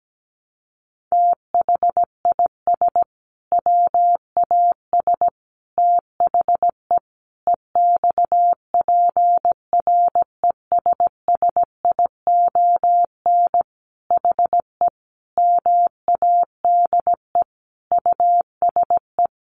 Morse code, heard as a keyed tone, about 17 wpm, 700 Hz. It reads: THIS WAS THE EXPRESSION HE MADE USE
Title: Sentences from Top 1200 Words 17wpm